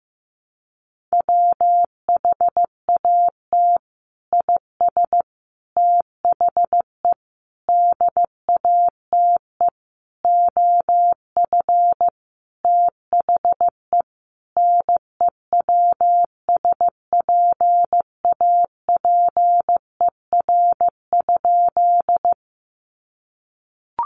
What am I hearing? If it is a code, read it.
Morse code, heard as a keyed tone, about 15 wpm, 700 Hz. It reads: WHAT IS THE DATE OF THE NEWSPAPER?
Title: Sentences from Top 1100 Words 15wpm